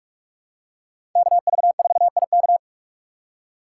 Morse code, heard as a keyed tone, about 41 wpm, 700 Hz. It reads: KV4IK
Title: USA Call Signs - 5 Characters - 6x Character Spacing - 45wpm